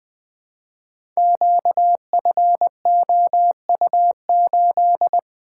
Morse code, heard as a keyed tone, about 20 wpm, 700 Hz. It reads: QFOU8
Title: Code Groups - 5 Letter:Numbers - 20wpm